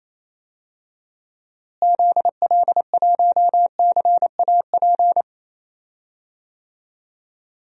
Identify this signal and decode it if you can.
Morse code, heard as a keyed tone, about 28 wpm, 700 Hz. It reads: ZL1CAP
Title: DXCC-100 Call Signs - 6 Characters - Repeated 2x - 28wpm